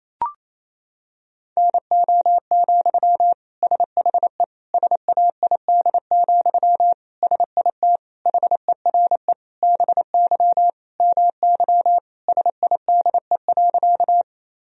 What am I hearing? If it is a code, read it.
Morse code, heard as a keyed tone, about 27 wpm, 700 Hz. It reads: NO, SHE SAID, SIT HERE BY MY SIDE.
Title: Sentences from Top 1000 Words - Review Part 1 - 28wpm